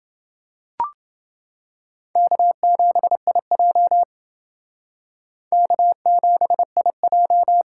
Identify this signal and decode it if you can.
Morse code, heard as a keyed tone, about 29 wpm, 700 Hz. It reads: K7IJ K7IJ